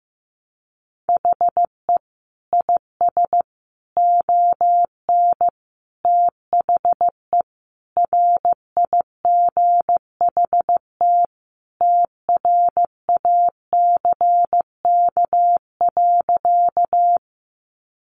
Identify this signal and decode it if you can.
Morse code, heard as a keyed tone, about 15 wpm, 700 Hz. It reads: HE IS ON THE RIGHT TRACK.
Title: Sentences from Top 1100 Words 15wpm